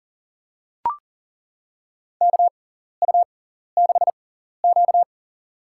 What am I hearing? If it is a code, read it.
Morse code, heard as a keyed tone, about 40 wpm, spaced as for 14 wpm, 700 Hz. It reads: KU6Q
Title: USA Call Signs - 4 Characters - 6x Character Spacing - 40wpm